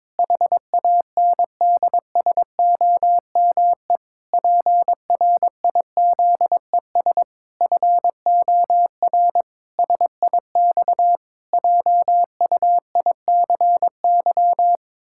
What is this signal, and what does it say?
Morse code, heard as a keyed tone, about 22 wpm, 700 Hz. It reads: HANDSOME PRIZES FOR SIX JUICY
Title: Pangram Challenge v2 - 22wpm